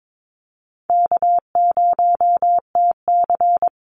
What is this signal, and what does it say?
Morse code, heard as a keyed tone, about 22 wpm, 700 Hz. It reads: K0TC
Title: Call Signs Repeated 3x - 22wpm